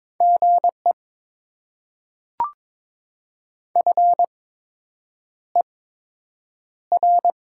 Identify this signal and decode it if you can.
Morse code, heard as a keyed tone, about 22 wpm, spaced as for 8 wpm, 700 Hz. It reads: GE FER
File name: Top 20 QSO Elements - 8x Character Spacing - 22wpm